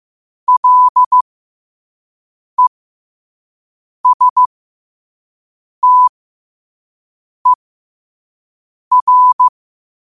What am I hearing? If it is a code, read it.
Morse code, heard as a keyed tone, about 15 wpm, spaced as for 5 wpm, 1000 Hz. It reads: LESTER
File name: qotdfiles / 5wpm